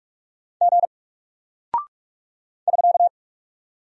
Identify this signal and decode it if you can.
Morse code, heard as a keyed tone, about 44 wpm, 700 Hz. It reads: G <SK>
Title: Single Character (Letter:Number:Punctuation:Prosign) Alt 45wpm